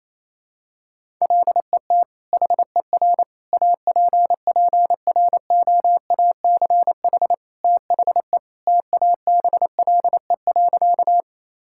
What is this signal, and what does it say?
Morse code, heard as a keyed tone, about 27 wpm, 700 Hz. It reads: LET HER APPROACH THE TABLE.
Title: Sentences from Top 1000 Words - Review Part 6 - 28wpm